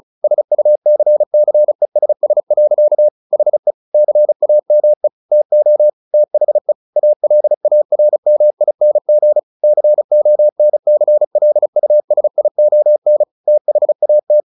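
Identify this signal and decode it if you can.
Morse code, heard as a keyed tone, about 34 wpm, 600 Hz. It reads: SUCCESS. HE CAME TO THE ALARMING CONCLUSION THAT
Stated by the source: Morse practice at 35 wpm from CW med Gnister, sending Ken Follett's novel Never in English.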